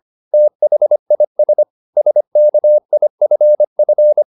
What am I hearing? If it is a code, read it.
Morse code, heard as a keyed tone, about 25 wpm, 600 Hz. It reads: THIS SKIFF
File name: the old man and the sea part 1 25wpm